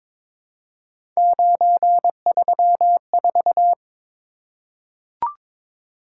Morse code, heard as a keyed tone, about 22 wpm, 700 Hz. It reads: 934
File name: practice / 22wpm / Three Letter-Number Combinations - 22wpm